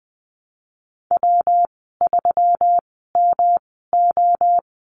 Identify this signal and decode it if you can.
Morse code, heard as a keyed tone, about 20 wpm, spaced as for 14 wpm, 700 Hz. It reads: W3MO
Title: 161 Lesson 26g - Letter g - Call Signs 20wpm